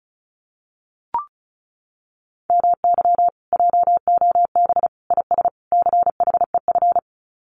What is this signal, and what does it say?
Morse code, heard as a keyed tone, about 36 wpm, 700 Hz. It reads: MY JOB IS CHEF
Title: Common QSO Phrases Encore 35wpm